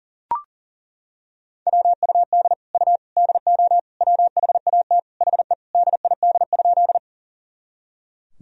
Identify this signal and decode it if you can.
Morse code, heard as a keyed tone, about 40 wpm, 700 Hz. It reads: WUD U DO WHAT HE DID?
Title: Sentences from Top 100 Words and Abbreviations 40wpm